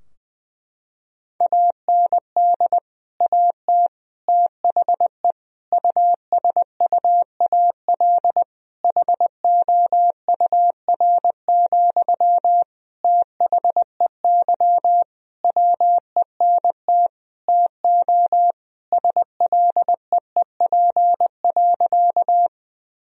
Morse code, heard as a keyed tone, about 20 wpm, 700 Hz. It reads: AND AT THE USUAL HOUR, THEY WENT TO SLEEP.
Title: Sentences from Top 1000 Words - Review Part 2 - 20wpm